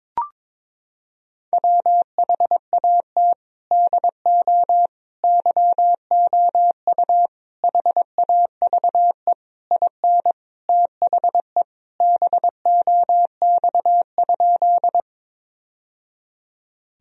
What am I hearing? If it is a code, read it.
Morse code, heard as a keyed tone, about 22 wpm, 700 Hz. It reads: WHAT DO YOU HAVE IN THE BOX?
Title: Sentences from Top 1000 Words - Review Part 2 - 22wpm